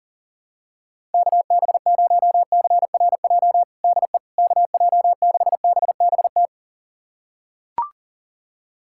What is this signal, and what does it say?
Morse code, heard as a keyed tone, about 40 wpm, 700 Hz. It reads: KB0CRJ DE KJ6BBT
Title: USA Callsign DE Callsign - 6 Characters - 40wpm